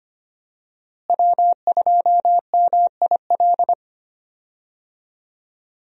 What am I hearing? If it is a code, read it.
Morse code, heard as a keyed tone, about 25 wpm, 700 Hz. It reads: W2MIL
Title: USA Call Signs - 5 Characters - Repeated 2x - 25wpm